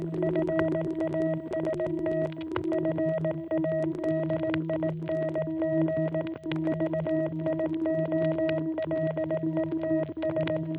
Morse code, heard as a keyed tone, about 18 wpm, 660 Hz. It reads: FAST FADING VIOLETS